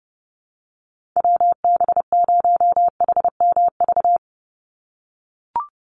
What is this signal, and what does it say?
Morse code, heard as a keyed tone, about 29 wpm, 700 Hz. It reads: WB0HMV
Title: USA Call Signs 30wpm